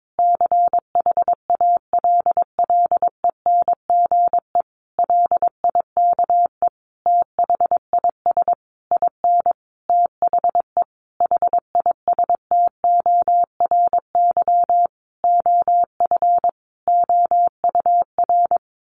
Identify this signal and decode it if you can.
Morse code, heard as a keyed tone, about 22 wpm, 700 Hz. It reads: CHALLENGE LIKE THIS IN THE HISTORY OF OUR
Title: Sentences from Top 1500 Words 22wpm